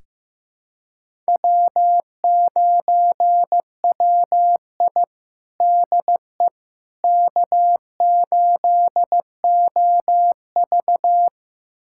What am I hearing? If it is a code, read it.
Morse code, heard as a keyed tone, about 16 wpm, 700 Hz. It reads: W9WI DE K8OV
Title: USA Callsign DE Callsign - 4 Characters - 15wpm